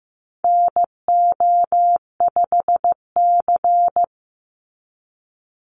Morse code, heard as a keyed tone, about 15 wpm, 700 Hz. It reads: NO5C
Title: USA Call Signs 15wpm